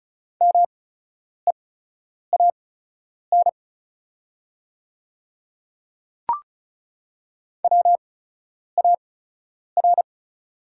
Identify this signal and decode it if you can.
Morse code, heard as a keyed tone, about 34 wpm, spaced as for 10 wpm, 690 Hz. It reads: MEAN WAR
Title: 8x Character Spacing - Top 500 Words - 35wpm